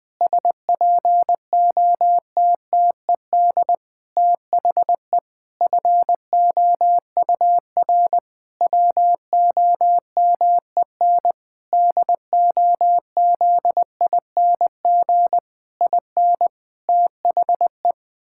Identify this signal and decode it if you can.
Morse code, heard as a keyed tone, about 20 wpm, 700 Hz. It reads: SPOTTED THE FOUR WOMEN DOZING IN THE
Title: Pangram Challenge v2 - 20wpm